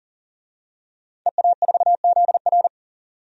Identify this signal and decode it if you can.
Morse code, heard as a keyed tone, about 40 wpm, 700 Hz. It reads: EA4ZR
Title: DXCC-100 Call Signs - 5 Characters - 40wpm